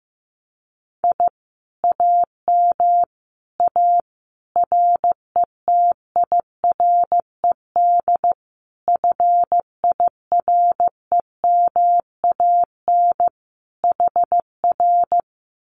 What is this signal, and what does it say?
Morse code, heard as a keyed tone, about 15 wpm, 700 Hz. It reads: I AM A RETIRED FIREMAN HR
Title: Common QSO Phrases Encore 15wpm